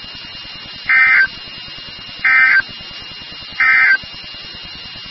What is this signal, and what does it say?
Emergency Alert System (SAME) tones, heard as harsh buzzing bursts. It is heard from 0.8 to 4.0 s.
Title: NOAA Weather Radio WXK36